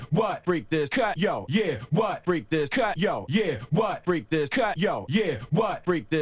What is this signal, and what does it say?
Speech. There is no code.